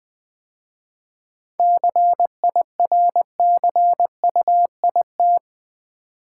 Morse code, heard as a keyed tone, about 20 wpm, 700 Hz. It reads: CIRCUIT
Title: 7 Letter Words - First 3 Letters Given - 20wpm